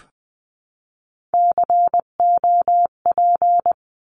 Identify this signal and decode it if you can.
Morse code, heard as a keyed tone, about 20 wpm, 700 Hz. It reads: COP
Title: 259 Lesson 42d - Words 18-20wpm